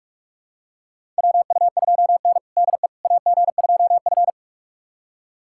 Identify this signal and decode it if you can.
Morse code, heard as a keyed tone, about 46 wpm, 700 Hz. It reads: WU2N DE AG2F
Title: USA Callsign DE Callsign - 4 Characters - 45wpm